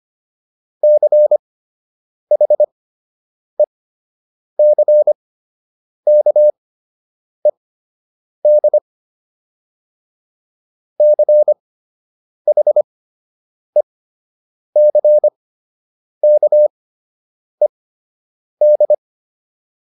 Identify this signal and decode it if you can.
Morse code, heard as a keyed tone, about 25 wpm, spaced as for 8 wpm, 600 Hz. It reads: CHECKED CHECKED